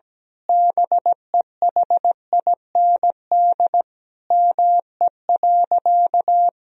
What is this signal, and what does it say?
Morse code, heard as a keyed tone, about 17 wpm, 700 Hz. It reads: BEHIND ME.
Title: Sentences from Top 1000 Words - Review Part 1 - 17wpm